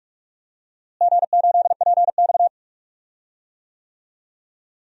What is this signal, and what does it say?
Morse code, heard as a keyed tone, about 44 wpm, 700 Hz. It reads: G8PX
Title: DXCC-100 Call Signs - 4 Characters - Repeated 3x - 45wpm